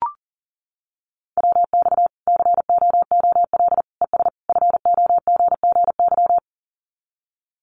Morse code, heard as a keyed tone, about 40 wpm, 700 Hz. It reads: WX COOL ES FOGGY
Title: Common QSO Phrases 40wpm